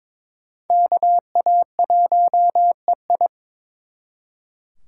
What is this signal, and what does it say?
Morse code, heard as a keyed tone, about 26 wpm, 700 Hz. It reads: KA1EI
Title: Speed Racing - USA 2x2 Call Signs 22wpm